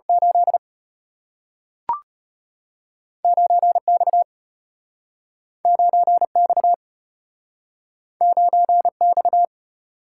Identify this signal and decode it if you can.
Morse code, heard as a keyed tone, about 33 wpm, 690 Hz. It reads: 8 9X 9X 9X